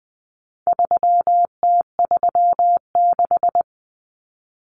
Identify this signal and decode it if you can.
Morse code, heard as a keyed tone, about 20 wpm, 700 Hz. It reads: 3T36
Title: Common Cut Numbers - 4-Digit 20wpm